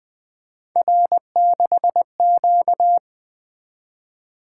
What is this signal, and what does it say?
Morse code, heard as a keyed tone, about 20 wpm, 700 Hz. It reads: R6Q
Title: Three Letter-Number Combinations - 20wpm